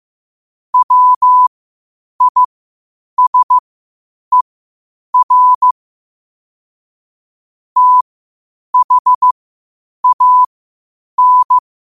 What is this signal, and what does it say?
Morse code, heard as a keyed tone, about 15 wpm, spaced as for 8 wpm, 1000 Hz. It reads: WISER THAN
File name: qotdfiles / 8wpm